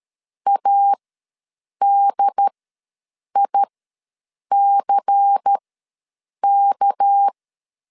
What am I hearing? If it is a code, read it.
Morse code, heard as a keyed tone, about 13 wpm, spaced as for 7 wpm, 800 Hz. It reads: ADICK
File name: LEZIONI / 8 YJ45 / 8 Y ws 12wpm 3409